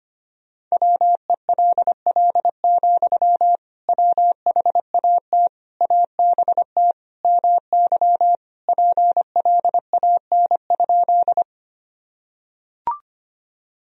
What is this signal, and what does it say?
Morse code, heard as a keyed tone, about 25 wpm, 700 Hz. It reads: WELL, WHAT ABT MY PLAN?
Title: Sentences from Top 500 Words and Abbreviations 25wpm